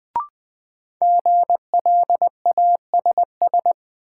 Morse code, heard as a keyed tone, about 20 wpm, 700 Hz. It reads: GLASS
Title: Sentences from Top 1000 Words - Review Part 3 - New Word Drill - 20wpm